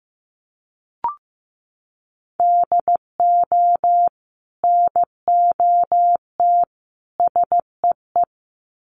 Morse code, heard as a keyed tone, about 15 wpm, 700 Hz. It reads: DO NOT SEE
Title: Sets of 3 Words 15wpm